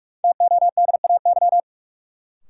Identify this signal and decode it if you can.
Morse code, heard as a keyed tone, about 42 wpm, 700 Hz. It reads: TODAY
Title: Sentences from Top 600 Words - New Word Drill - 45wpm